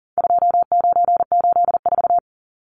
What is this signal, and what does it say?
Morse code, heard as a keyed tone, about 40 wpm, 700 Hz. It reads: 2984
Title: Numbers - 4-Digit 40wpm